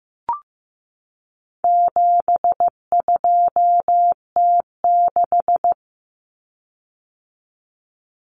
Morse code, heard as a keyed tone, about 14 wpm, 700 Hz. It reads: 72T6
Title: Common Cut Numbers - 4-Digit 15wpm